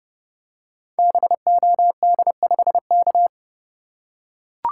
Morse code, heard as a keyed tone, about 30 wpm, 700 Hz. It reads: BOD5K
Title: Code Groups - 5 Letter:Numbers - 30wpm